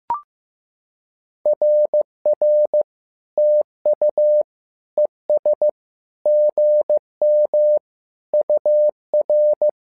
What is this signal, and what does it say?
Morse code, heard as a keyed tone, about 15 wpm, 600 Hz. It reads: RR TU ES GM UR
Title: SOTA QSO Practice 15wpm